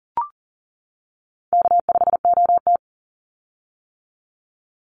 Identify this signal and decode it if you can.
Morse code, heard as a keyed tone, about 38 wpm, 700 Hz. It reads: K5OT
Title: Easy Call Signs - 40wpm